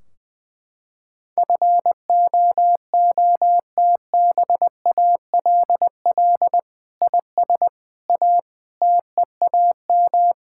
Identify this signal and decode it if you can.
Morse code, heard as a keyed tone, about 20 wpm, 700 Hz. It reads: FOOTBALL IS A TEAM
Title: Sentences from Top 1600 Words 20wpm